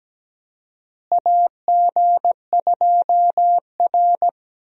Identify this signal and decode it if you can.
Morse code, heard as a keyed tone, about 17 wpm, 700 Hz. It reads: AG2R
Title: USA Call Signs - 4 Characters - Repeated 3x - 17wpm